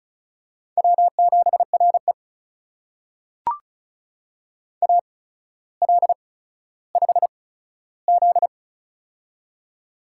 Morse code, heard as a keyed tone, about 36 wpm, spaced as for 15 wpm, 700 Hz. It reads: W7RE AL5Z
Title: USA Call Signs - 4 Characters - 8x Character Spacing - 35wpm